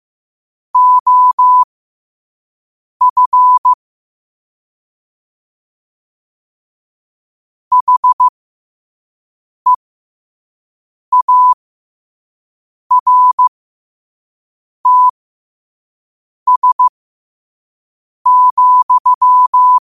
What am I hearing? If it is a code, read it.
Morse code, heard as a keyed tone, about 15 wpm, spaced as for 5 wpm, 1000 Hz. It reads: OF HEARTS,
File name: qotdfiles / 5wpm